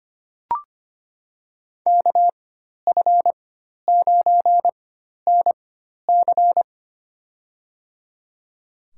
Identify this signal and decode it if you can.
Morse code, heard as a keyed tone, about 25 wpm, spaced as for 12 wpm, 700 Hz. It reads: KF9NC
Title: USA Call Signs - 5 Characters - 4x Character Spacing - 25wpm